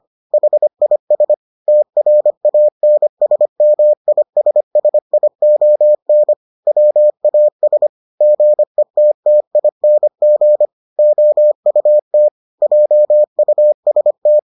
Morse code, heard as a keyed tone, about 25 wpm, 600 Hz. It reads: HIS TRANSMISSION WAS GETTING OUT JUST